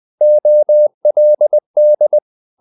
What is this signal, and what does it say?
Morse code, heard as a keyed tone, about 20 wpm, 600 Hz. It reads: OLD
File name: the old man and the sea part 2 20wpm